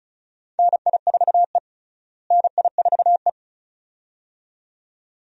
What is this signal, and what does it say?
Morse code, heard as a keyed tone, about 36 wpm, 700 Hz. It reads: NI4E NI4E
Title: USA Call Signs - 4 Characters - Repeated 2x - 35wpm